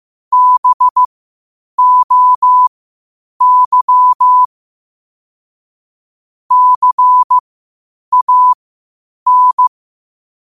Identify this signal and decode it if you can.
Morse code, heard as a keyed tone, about 15 wpm, spaced as for 8 wpm, 1000 Hz. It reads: BOY CAN
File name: qotdfiles / 8wpm